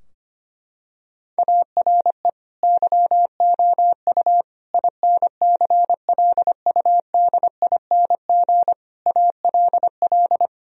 Morse code, heard as a keyed tone, about 25 wpm, 700 Hz. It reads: ARE YOU INCLUDING ALL
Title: Sentences from Top 1000 Words - Review Part 4 - 25wpm